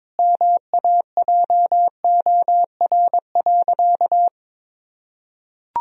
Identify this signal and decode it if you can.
Morse code, heard as a keyed tone, about 22 wpm, 700 Hz. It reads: MAJOR.
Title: Sentences from Top 1000 Words - Review Part 4 - 22wpm